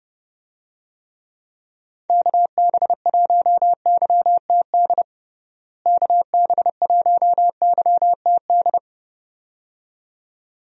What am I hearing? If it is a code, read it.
Morse code, heard as a keyed tone, about 30 wpm, 700 Hz. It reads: KB1YTD KB1YTD
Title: USA Call Signs - 6 Characters - Repeated 2x - 30wpm